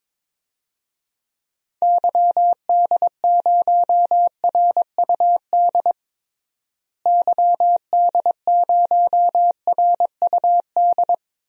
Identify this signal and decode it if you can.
Morse code, heard as a keyed tone, about 22 wpm, 700 Hz. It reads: YD0RUD YD0RUD
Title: DXCC-100 Call Signs - 6 Characters - Repeated 2x - 22wpm